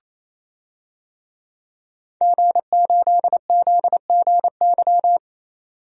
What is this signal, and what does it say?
Morse code, heard as a keyed tone, about 28 wpm, 700 Hz. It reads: G8ZGY
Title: DXCC-100 Call Signs - 5 Characters - 28wpm